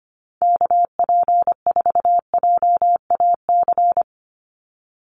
Morse code, heard as a keyed tone, about 25 wpm, 700 Hz. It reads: KP4JAC
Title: USA Call Signs 25wpm